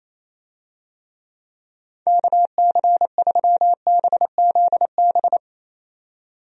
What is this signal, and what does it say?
Morse code, heard as a keyed tone, about 27 wpm, 700 Hz. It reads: KC3BZB
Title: USA Call Signs - 6 Characters - 28wpm